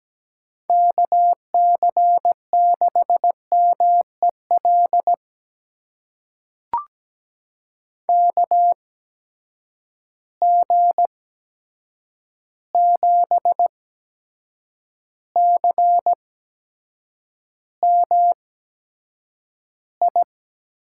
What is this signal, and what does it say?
Morse code, heard as a keyed tone, about 17 wpm, spaced as for 7 wpm, 700 Hz. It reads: KC6MEL KG7CMI
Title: USA Call Signs - 6 Characters - 8x Character Spacing - 17wpm